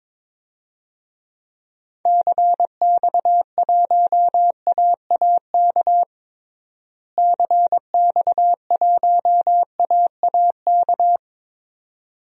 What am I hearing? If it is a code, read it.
Morse code, heard as a keyed tone, about 22 wpm, 700 Hz. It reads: CX1AAK CX1AAK
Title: DXCC-100 Call Signs - 6 Characters - Repeated 2x - 22wpm